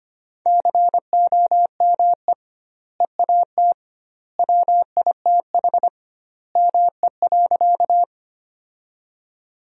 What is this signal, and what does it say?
Morse code, heard as a keyed tone, about 25 wpm, 700 Hz. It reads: COME EAT WITH ME.
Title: 2x Word-Spacing - Sentences from Top 300 Words - 25wpm